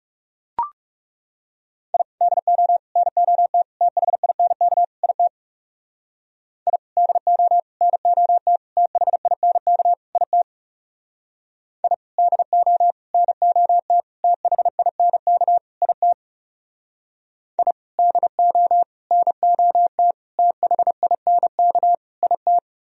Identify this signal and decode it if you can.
Morse code, heard as a keyed tone, about 36 wpm, 700 Hz. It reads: I DO NOT THINK IT I DO NOT THINK IT I DO NOT THINK IT I DO NOT THINK IT
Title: Speed-Racing - Sets of 5 Words 30wpm